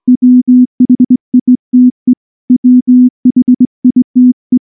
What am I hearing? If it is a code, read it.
Morse code, heard as a keyed tone, about 20 wpm, 250 Hz. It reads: WHITE WHITE
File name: morse / morse-training-20WPM-6